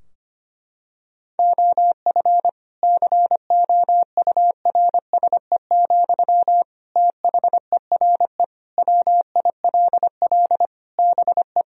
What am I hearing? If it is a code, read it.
Morse code, heard as a keyed tone, about 25 wpm, 700 Hz. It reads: OF COURSE, THERE WILL BE
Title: Sentences from Top 1000 Words - Review Part 4 - 25wpm